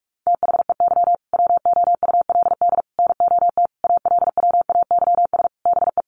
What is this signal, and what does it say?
Morse code, heard as a keyed tone, about 44 wpm, 700 Hz. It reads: THEY WOULD NOT ALWAYS BE
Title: Sentences from Top 1400 Words 45wpm